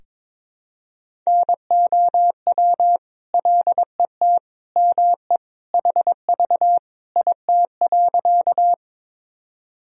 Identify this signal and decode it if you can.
Morse code, heard as a keyed tone, about 22 wpm, 700 Hz. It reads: NOW LET ME HV IT.
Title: Sentences from Top 300 Words and Abbreviations 22wpm